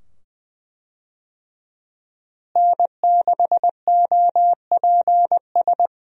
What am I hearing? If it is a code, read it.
Morse code, heard as a keyed tone, about 20 wpm, 700 Hz. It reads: N6OPS
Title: USA Call Signs - 5 Characters - Repeated 2x - 20wpm